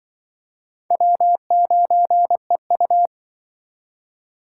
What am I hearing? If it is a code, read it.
Morse code, heard as a keyed tone, about 24 wpm, 700 Hz. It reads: W9EU
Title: Speed Racing - USA 1x2 Call Signs 20wpm